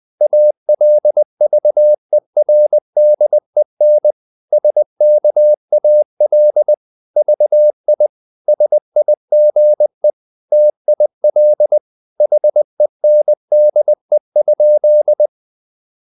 Morse code, heard as a keyed tone, about 20 wpm, 600 Hz. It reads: ALVERDEN SKAL VI SIGE TIL HENDE?
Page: Ferie 20 wpm